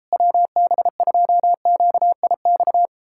Morse code, heard as a keyed tone, about 38 wpm, 700 Hz. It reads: WB2QIX